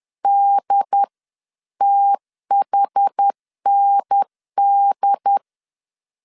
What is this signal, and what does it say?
Morse code, heard as a keyed tone, about 10 wpm, 800 Hz. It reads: DTHND
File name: CW / DET / 1 3 1500C S B 10wpm 3335